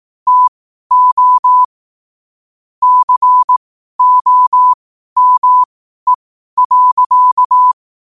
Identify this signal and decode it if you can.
Morse code, heard as a keyed tone, about 18 wpm, spaced as for 12 wpm, 1000 Hz. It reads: TO COME.
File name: qotdfiles / 12wpm